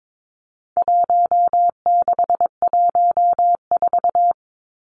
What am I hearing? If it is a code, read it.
Morse code, heard as a keyed tone, about 22 wpm, 700 Hz. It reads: 1614
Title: Numbers - Contest Style - 4-Digit 22wpm